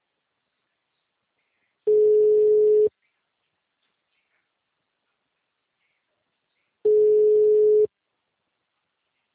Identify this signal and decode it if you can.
Ringback tone (Europe, 425 Hz).
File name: out- -2019- - - .72669